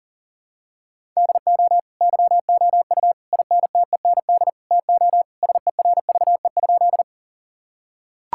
Morse code, heard as a keyed tone, about 40 wpm, 700 Hz. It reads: DO YOU INTEND TO SERVE?
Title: Sentences from Top 1600 Words 40wpm